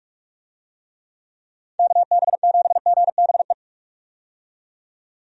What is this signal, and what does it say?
Morse code, heard as a keyed tone, about 43 wpm, 700 Hz. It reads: KB7GBE
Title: USA Call Signs - 6 Characters - 45wpm